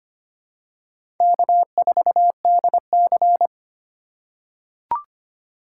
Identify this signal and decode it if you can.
Morse code, heard as a keyed tone, about 25 wpm, 700 Hz. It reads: K4DC
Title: USA Call Signs - 2x Character Spacing - 25wpm